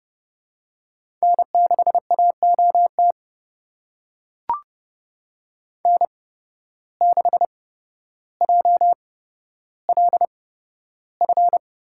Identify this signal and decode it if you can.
Morse code, heard as a keyed tone, about 30 wpm, spaced as for 13 wpm, 700 Hz. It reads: N6AOT N6JLF
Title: USA Call Signs - 5 Characters - 8x Character Spacing - 30wpm